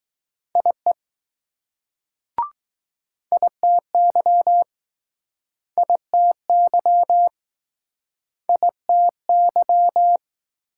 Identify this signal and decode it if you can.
Morse code, heard as a keyed tone, about 20 wpm, 700 Hz. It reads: IE ITY ITY ITY